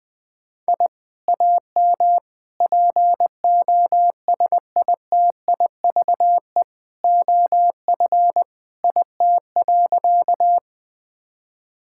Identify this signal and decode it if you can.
Morse code, heard as a keyed tone, about 20 wpm, 700 Hz. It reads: I AM POSITIVE OF IT.
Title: Sentences from Top 1000 Words - Review Part 5 - 20wpm